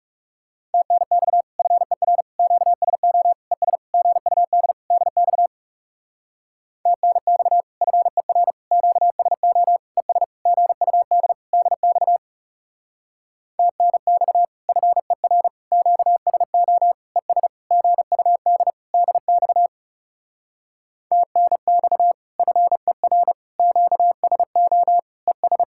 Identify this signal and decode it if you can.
Morse code, heard as a keyed tone, about 38 wpm, 700 Hz. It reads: TNX FER QSO ES GUD DX TNX FER QSO ES GUD DX TNX FER QSO ES GUD DX TNX FER QSO ES